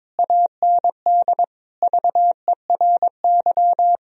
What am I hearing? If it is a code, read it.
Morse code, heard as a keyed tone, about 22 wpm, 700 Hz. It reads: AND VERY
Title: Sentences from Top 1000 Words - Review Part 1 - 22wpm